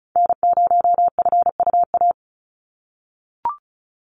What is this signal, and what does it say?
Morse code, heard as a keyed tone, about 34 wpm, 700 Hz. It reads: N0FUA
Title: USA Call Signs 35wpm